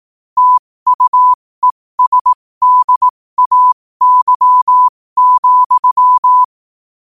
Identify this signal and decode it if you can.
Morse code, heard as a keyed tone, about 18 wpm, spaced as for 15 wpm, 1000 Hz. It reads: TUESDAY,
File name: qotdfiles / 15wpm